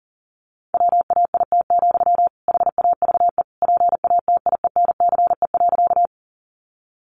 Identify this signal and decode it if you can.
Morse code, heard as a keyed tone, about 40 wpm, 700 Hz. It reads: WAIT, HAVE PATIENCE.
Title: Sentences from Top 1300 Words 40wpm